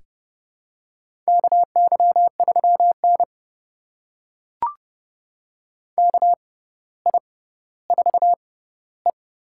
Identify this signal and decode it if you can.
Morse code, heard as a keyed tone, about 28 wpm, spaced as for 15 wpm, 700 Hz. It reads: KY3N KI4E
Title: USA Call Signs - 6x Character Spacing - 30wpm